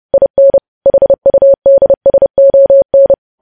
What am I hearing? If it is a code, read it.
Morse code, heard as a keyed tone, about 30 wpm, 550 Hz. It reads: IN HUDSON
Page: news headlines in morse code at 30 wpm